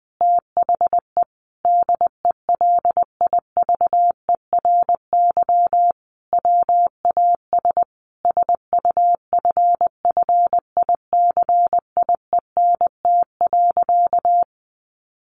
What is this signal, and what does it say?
Morse code, heard as a keyed tone, about 20 wpm, 700 Hz. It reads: THE DELIVERY WAS SUFFICIENT.
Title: Sentences from Top 1500 Words 20wpm